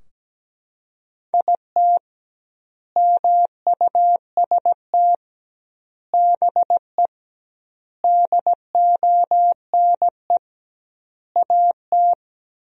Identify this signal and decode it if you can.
Morse code, heard as a keyed tone, about 17 wpm, 700 Hz. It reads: IT MUST BE DONE AT
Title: 2x Word-Spacing - Sentences from Top 500 Words - 17wpm